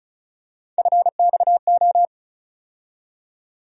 Morse code, heard as a keyed tone, about 36 wpm, 700 Hz. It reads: FXO
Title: USA Call Sign Postfixes - 3 Letters 35wpm